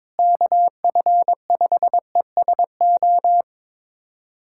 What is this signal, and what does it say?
Morse code, heard as a keyed tone, about 22 wpm, 700 Hz. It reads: KF5ESO
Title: USA Callsign DE Callsign - 6 Characters - 22wpm